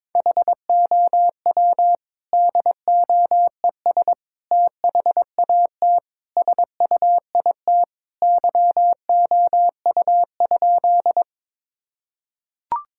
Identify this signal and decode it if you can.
Morse code, heard as a keyed tone, about 22 wpm, 700 Hz. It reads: HOW DOES THAT SUIT YOU?
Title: Sentences from Top 1600 Words 22wpm